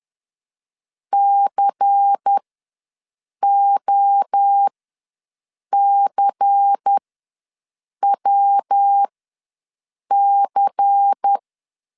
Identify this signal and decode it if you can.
Morse code, heard as a keyed tone, about 10 wpm, spaced as for 6 wpm, 800 Hz. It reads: COCWC
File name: LEZIONI / 5 LBVC / 5 LBVC ws 10wpm 3733